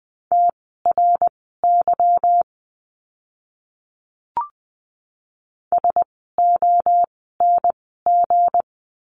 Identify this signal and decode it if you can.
Morse code, heard as a keyed tone, about 20 wpm, spaced as for 15 wpm, 700 Hz. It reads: TRY SONG
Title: 160 Lesson 26f - Letter g - Words 20wpm